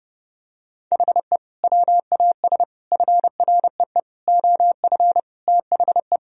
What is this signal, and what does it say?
Morse code, heard as a keyed tone, about 32 wpm, 700 Hz. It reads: HE WAS FREE OF THE
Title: Sentences from Top 1000 Words - Review Part 4 - 30wpm